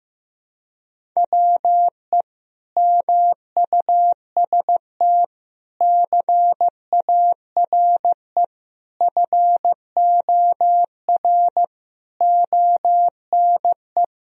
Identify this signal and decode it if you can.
Morse code, heard as a keyed tone, about 15 wpm, 700 Hz. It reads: WE MUST CARE FOR ONE
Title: Sentences from Top 1000 Words - Review Part 1 - 15wpm